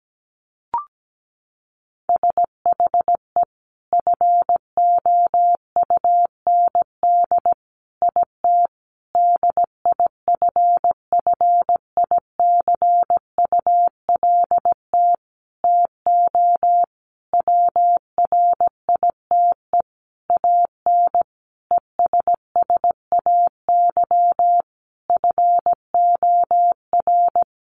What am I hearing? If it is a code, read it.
Morse code, heard as a keyed tone, about 17 wpm, 700 Hz. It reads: SHE FOUND IT DIFFICULT TO WRITE AN ESSAY FOR
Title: Sentences from Top 1500 Words 17wpm